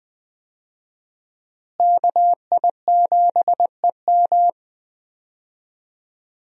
Morse code, heard as a keyed tone, about 20 wpm, 700 Hz. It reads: KI7EM
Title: USA Call Signs - 5 Characters - Repeated 2x - 20wpm